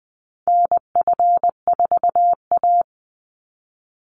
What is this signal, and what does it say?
Morse code, heard as a keyed tone, about 19 wpm, 700 Hz. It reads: NF4A
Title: Call Signs Repeated 3x - 20wpm